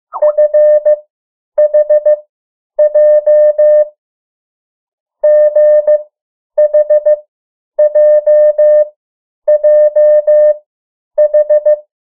Morse code, heard as a keyed tone, about 15 wpm, spaced as for 9 wpm, 600 Hz. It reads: F H J GH J J H
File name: morse mp3 / 03 urok 2